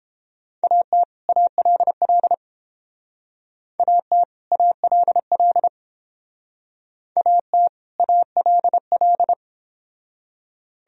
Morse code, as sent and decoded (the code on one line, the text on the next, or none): .- - / .- .-.. .-.. / .- - / .- .-.. .-.. / .- - / .- .-.. .-..
AT ALL AT ALL AT ALL